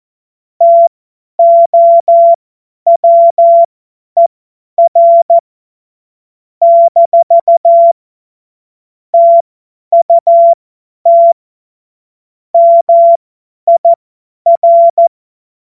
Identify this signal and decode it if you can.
Morse code, heard as a keyed tone, about 14 wpm, spaced as for 10 wpm, 680 Hz. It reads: TOWER - TUT MIR
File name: Morsekurs / Witze / Witz 25. April 2023 50 F70